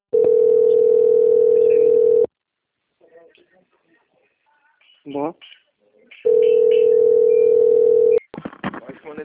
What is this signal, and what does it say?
Ringback tone (North America), 440 + 480 Hz, with speech.
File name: out- - - - - .3493